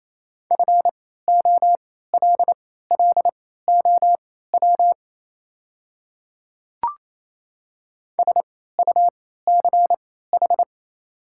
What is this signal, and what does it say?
Morse code, heard as a keyed tone, about 27 wpm, spaced as for 16 wpm, 700 Hz. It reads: FOLLOW SUCH
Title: 3x Character Spacing - Top 300 Words - 28wpm